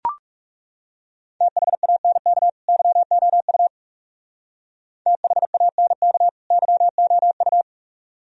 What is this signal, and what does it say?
Morse code, heard as a keyed tone, about 36 wpm, 700 Hz. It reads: THANK YOU THANK YOU
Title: Speed-Racing - Sets of 2 Words 30wpm